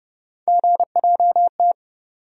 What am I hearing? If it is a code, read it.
Morse code, heard as a keyed tone, about 30 wpm, 700 Hz. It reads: GJT
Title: USA Call Sign Postfixes - 3 Letters 30wpm